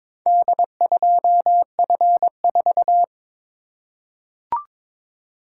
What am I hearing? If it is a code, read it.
Morse code, heard as a keyed tone, about 22 wpm, 700 Hz. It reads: D2F4
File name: practice / 22wpm / Four Letter-Number Combinations - 22wpm